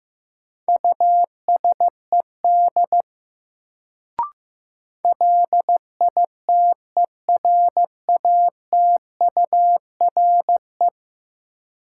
Morse code, heard as a keyed tone, about 15 wpm, 700 Hz. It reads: USED LITERATURE
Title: Sentences from Top 600 Words - New Word Drill - 15wpm